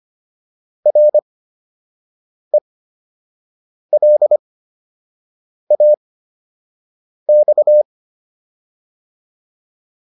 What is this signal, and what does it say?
Morse code, heard as a keyed tone, about 25 wpm, spaced as for 6 wpm, 600 Hz. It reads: RELAX